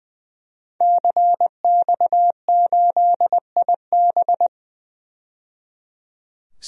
Morse code, heard as a keyed tone, about 20 wpm, 700 Hz. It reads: CX8IB
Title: DXCC-100 Call Signs - 5 Characters - 20wpm